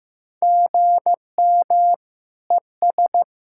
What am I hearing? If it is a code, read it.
Morse code, heard as a keyed tone, about 15 wpm, 700 Hz. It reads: GM ES
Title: Practice QSO - Exchange 1 - Beginning - 15wpm